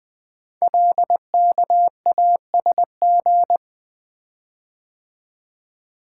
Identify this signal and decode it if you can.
Morse code, heard as a keyed tone, about 20 wpm, 700 Hz. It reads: LKASG
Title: Code Groups - 5 Letters - 20wpm